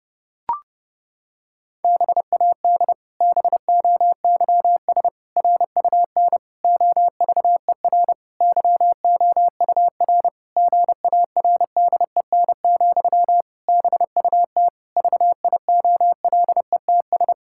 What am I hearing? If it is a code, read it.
Morse code, heard as a keyed tone, about 30 wpm, 700 Hz. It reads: BAD BOYS RUN OVER YOUR GARDEN, BUT VIOLETS
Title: Pangram Challenge v2 - 30wpm